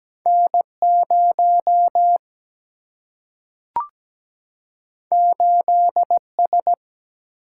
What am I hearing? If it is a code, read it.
Morse code, heard as a keyed tone, about 17 wpm, 700 Hz. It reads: N0 8S
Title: Single-Letter Single-Number 17wpm